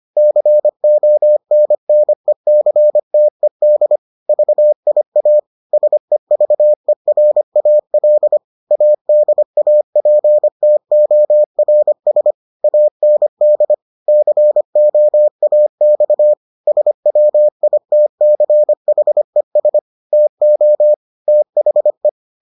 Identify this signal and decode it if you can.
Morse code, heard as a keyed tone, about 25 wpm, 600 Hz. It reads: CONNECTED VIA SEVERAL ADAPTORS AND COAX SWITCHES TO THE